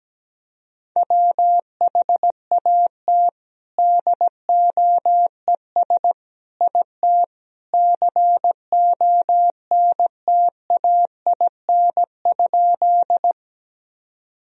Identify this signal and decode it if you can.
Morse code, heard as a keyed tone, about 17 wpm, 700 Hz. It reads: WHAT DOES IT CONTAIN?
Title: Sentences from Top 1000 Words - Review Part 2 - 17wpm